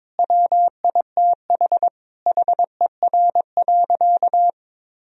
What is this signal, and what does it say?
Morse code, heard as a keyed tone, about 22 wpm, 700 Hz. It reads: WITH HER.
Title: Sentences from Top 1000 Words - Review Part 3 - 22wpm